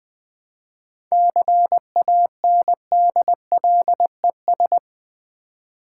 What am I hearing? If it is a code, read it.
Morse code, heard as a keyed tone, about 20 wpm, 700 Hz. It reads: CANDLES
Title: Easy Words with Hints - 20wpm